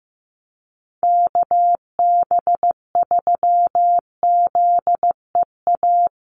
Morse code, heard as a keyed tone, about 15 wpm, 700 Hz. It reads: KB3ZEA
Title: USA Call Signs Repeated 3x 15wpm